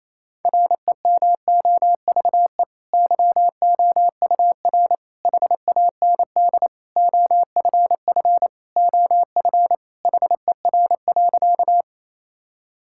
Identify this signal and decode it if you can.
Morse code, heard as a keyed tone, about 28 wpm, 700 Hz. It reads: REMOVE YOUR HAND OFF OF HER.
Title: Sentences from Top 1000 Words - Review Part 5 - 28wpm